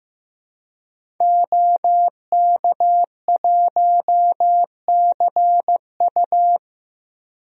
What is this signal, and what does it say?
Morse code, heard as a keyed tone, about 15 wpm, 700 Hz. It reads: OK1CU
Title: DXCC-100 Call Signs - 5 Characters - Repeated 2x - 15wpm